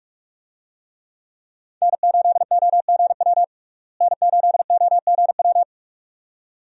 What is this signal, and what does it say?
Morse code, heard as a keyed tone, about 41 wpm, 700 Hz. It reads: N8OGW N8OGW
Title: USA Call Signs - 5 Characters - Repeated 2x - 45wpm